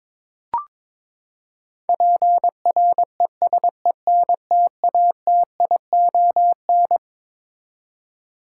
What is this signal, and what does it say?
Morse code, heard as a keyed tone, about 22 wpm, 700 Hz. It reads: PRESENTATION
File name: Twelve Letter Words - 22wpm